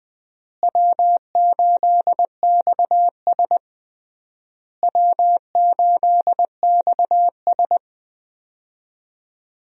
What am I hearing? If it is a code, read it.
Morse code, heard as a keyed tone, about 20 wpm, 700 Hz. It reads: W8XS W8XS
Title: USA Call Signs - 4 Characters - Repeated 3x - 20wpm